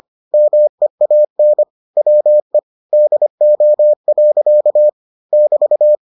Morse code, heard as a keyed tone, about 25 wpm, 600 Hz. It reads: MEAN WE DO. <BT>